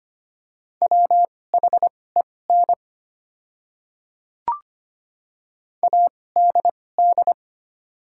Morse code, heard as a keyed tone, about 25 wpm, spaced as for 18 wpm, 700 Hz. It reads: WHEN ADD
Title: 2x Character Spacing - Top 200 Words - 25wpm